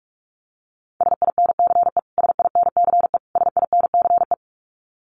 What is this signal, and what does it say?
Morse code, heard as a keyed tone, about 46 wpm, 700 Hz. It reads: SINCE SINCE SINCE
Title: Top 500 Words Repeated 45wpm